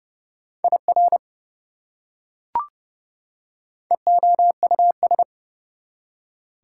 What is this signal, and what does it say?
Morse code, heard as a keyed tone, about 29 wpm, 700 Hz. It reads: IR EOUS
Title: Prefixes and Suffixes 30wpm